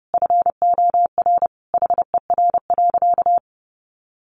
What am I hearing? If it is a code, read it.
Morse code, heard as a keyed tone, about 30 wpm, 700 Hz. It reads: FOR HER.
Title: Sentences from Top 600 Words 30wpm